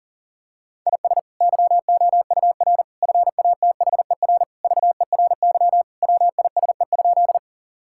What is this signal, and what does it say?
Morse code, heard as a keyed tone, about 40 wpm, 700 Hz. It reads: IS YOUR FATHER VERY WISE?
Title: Sentences from Top 1600 Words 40wpm